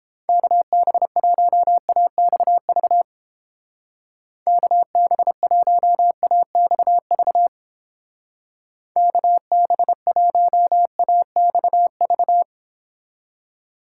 Morse code, heard as a keyed tone, about 29 wpm, 700 Hz. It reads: KB1AXV KB1AXV KB1AXV